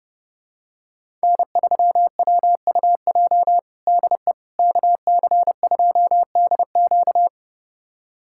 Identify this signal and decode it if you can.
Morse code, heard as a keyed tone, about 32 wpm, 700 Hz. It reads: N3WUJ DE KC2DQ
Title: USA Callsign DE Callsign - 5 Characters - 30wpm